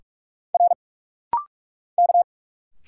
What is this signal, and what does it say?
Morse code, heard as a keyed tone, about 46 wpm, 700 Hz. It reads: R K
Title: Single Letters - Rapid-Fire 45wpm